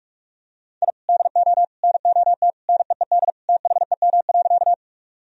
Morse code, heard as a keyed tone, about 42 wpm, 700 Hz. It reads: I DO NOT NEED THEM.
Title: Sentences from Top 1000 Words - Review Part 1 - 45wpm